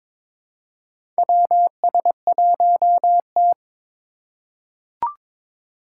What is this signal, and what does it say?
Morse code, heard as a keyed tone, about 21 wpm, 700 Hz. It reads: WS1T
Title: USA Call Signs - 4 Characters - 3x Character Spacing - 22wpm